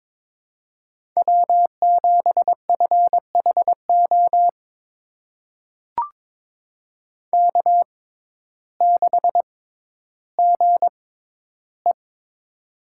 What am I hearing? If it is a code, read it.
Morse code, heard as a keyed tone, about 22 wpm, spaced as for 12 wpm, 700 Hz. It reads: W7FHO K6GE
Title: USA Call Signs - 6x Character Spacing - 22wpm